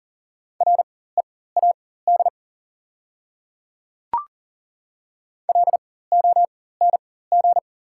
Morse code, heard as a keyed tone, about 38 wpm, spaced as for 18 wpm, 690 Hz. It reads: READ LONG
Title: 4x Character Spacing - Top 200 Words - 40wpm